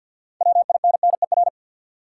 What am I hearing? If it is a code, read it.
Morse code, heard as a keyed tone, about 49 wpm, 700 Hz. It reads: WINNER
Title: Six Letter Words - 50wpm